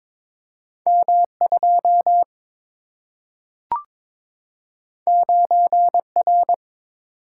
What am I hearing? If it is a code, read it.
Morse code, heard as a keyed tone, about 22 wpm, 700 Hz. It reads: M 2 9 R
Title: Single-Letter Single-Number 22wpm